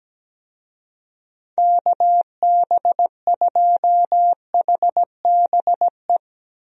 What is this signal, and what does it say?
Morse code, heard as a keyed tone, about 17 wpm, 700 Hz. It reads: KB2HBE
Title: USA Call Signs - 6 Characters - 17wpm